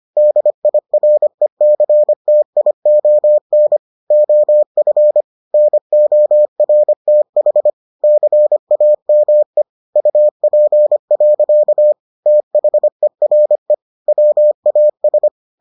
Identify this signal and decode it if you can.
Morse code, heard as a keyed tone, about 25 wpm, 600 Hz. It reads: DIRECTION OF NORTH CAME UP. THERE WAS